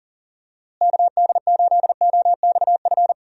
Morse code, heard as a keyed tone, about 39 wpm, 700 Hz. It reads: KD8OXF
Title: USA Call Signs - 6 Characters - Repeated 3x - 40wpm